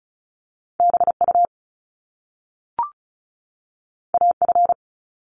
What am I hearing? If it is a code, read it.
Morse code, heard as a keyed tone, about 36 wpm, 700 Hz. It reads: BU AF
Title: Top Two Letter Combinations in Words 35wpm